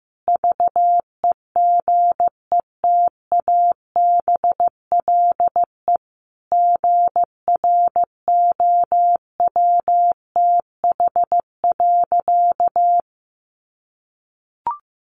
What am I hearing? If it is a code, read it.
Morse code, heard as a keyed tone, about 15 wpm, 700 Hz. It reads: VEGETABLE GROWTH.
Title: Sentences from Top 1300 Words 15wpm